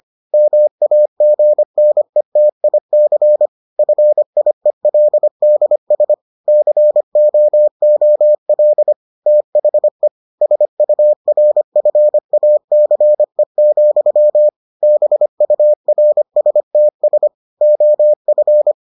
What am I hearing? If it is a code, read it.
Morse code, heard as a keyed tone, about 25 wpm, 600 Hz. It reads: MAGNETIC FIELDS COOL THE SURFACE, BURSTS OF